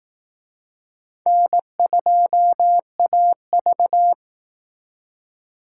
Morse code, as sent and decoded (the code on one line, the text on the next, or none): -. ..--- .- ...-
N2AV